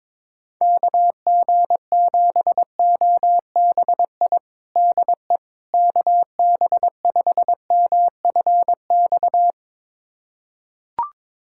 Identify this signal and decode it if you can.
Morse code, heard as a keyed tone, about 22 wpm, 700 Hz. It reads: KG7OBI DE KB5MFX
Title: USA Callsign DE Callsign - 6 Characters - 22wpm